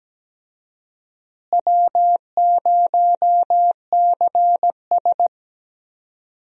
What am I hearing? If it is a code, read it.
Morse code, heard as a keyed tone, about 17 wpm, 700 Hz. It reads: W0CS
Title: USA Call Signs - 4 Characters - Repeated 2x - 17wpm